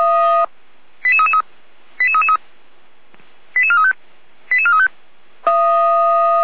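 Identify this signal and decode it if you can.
Five-tone selective call (ZVEI), heard as quick runs of five tones, tones of about 70 ms. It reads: 88333 88435 88435
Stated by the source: ZVEI five-tone selective call received via SDR, from the Feuerwehr Orlamünde audio archive.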